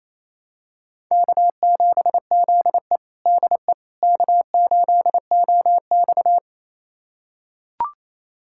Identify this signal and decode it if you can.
Morse code, heard as a keyed tone, about 28 wpm, 700 Hz. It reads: K7ZE DE K8OX
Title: USA Callsign DE Callsign - 4 Characters - 28wpm